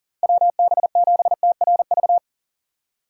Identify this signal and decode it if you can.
Morse code, heard as a keyed tone, about 40 wpm, 700 Hz. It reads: WB7TRV
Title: USA Call Signs - 6 Characters - 3x Character Spacing - 40wpm